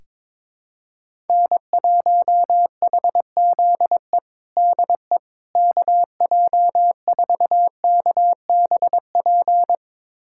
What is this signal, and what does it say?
Morse code, heard as a keyed tone, about 22 wpm, 700 Hz. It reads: N1HZE DE KJ4KBP
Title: USA Callsign DE Callsign 22wpm